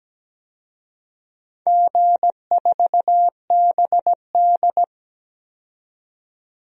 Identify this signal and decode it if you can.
Morse code, heard as a keyed tone, about 17 wpm, 700 Hz. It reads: G4BD
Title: DXCC-100 Call Signs - 4 Characters - 17wpm